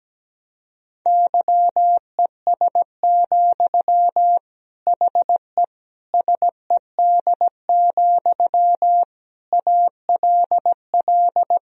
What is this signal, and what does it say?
Morse code, heard as a keyed tone, about 17 wpm, 700 Hz. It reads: YES, HE SED, ALL
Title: Sentences from Top 500 Words and Abbreviations 17wpm